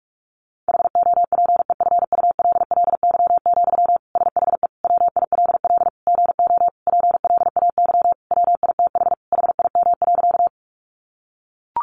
Morse code, heard as a keyed tone, about 44 wpm, 700 Hz. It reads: HOPEFULLY, SHE WILL GO PLAY WITH HIM.
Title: Sentences from Top 1500 Words 45wpm